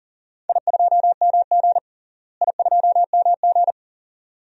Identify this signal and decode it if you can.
Morse code, heard as a keyed tone, about 40 wpm, 700 Hz. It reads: I2MG I2MG
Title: DXCC-100 Call Signs - 4 Characters - Repeated 2x - 40wpm